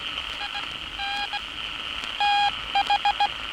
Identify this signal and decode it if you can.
Morse code, heard as a keyed tone, about 16 wpm, spaced as for 14 wpm, 810 Hz. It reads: IN TH
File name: training / NAVPERS 11049to11060 / NAVPERS 11059 RA RB